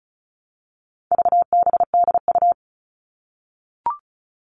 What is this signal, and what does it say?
Morse code, heard as a keyed tone, about 34 wpm, 700 Hz. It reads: VBDU
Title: All Cut Numbers - 4-Digit 35wpm